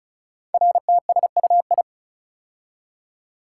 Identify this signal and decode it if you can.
Morse code, heard as a keyed tone, about 36 wpm, 700 Hz. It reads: RTSUI
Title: Code Groups - 5 Letters - 35wpm